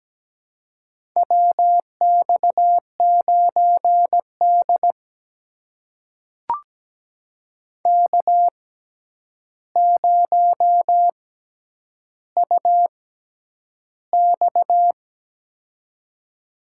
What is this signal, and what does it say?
Morse code, heard as a keyed tone, about 17 wpm, spaced as for 9 wpm, 700 Hz. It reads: WX9D K0UX
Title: USA Call Signs - 4 Characters - 6x Character Spacing - 17wpm